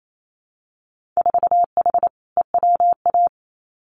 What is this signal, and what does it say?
Morse code, heard as a keyed tone, about 28 wpm, 700 Hz. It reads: 4H EWA
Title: Winter Field Day Exchanges Encore 28wpm